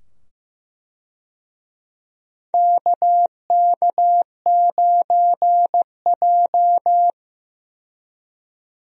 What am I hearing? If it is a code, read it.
Morse code, heard as a keyed tone, about 15 wpm, 700 Hz. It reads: KK9J
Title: USA Call Signs - 4 Characters - 15wpm